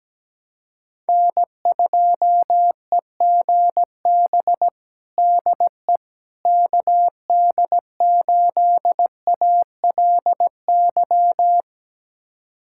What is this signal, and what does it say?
Morse code, heard as a keyed tone, about 17 wpm, 700 Hz. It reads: N2EGB DE KD8ALY
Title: USA Callsign DE Callsign 17wpm